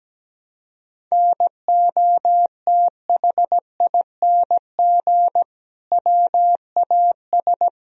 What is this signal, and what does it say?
Morse code, heard as a keyed tone, about 17 wpm, 700 Hz. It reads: NOTHING WAS
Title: Sentences from Top 1000 Words - Review Part 2 - 17wpm